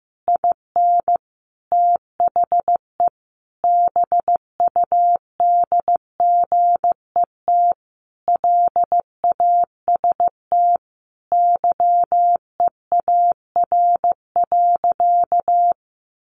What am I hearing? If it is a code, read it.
Morse code, heard as a keyed tone, about 15 wpm, 700 Hz. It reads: IN THE BUDGET LAST YEAR.
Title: Sentences from Top 1200 Words 15wpm